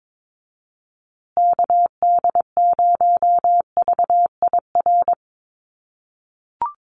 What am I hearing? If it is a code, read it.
Morse code, heard as a keyed tone, about 22 wpm, 700 Hz. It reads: KD0VIR
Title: USA Call Signs Repeated 3x 22wpm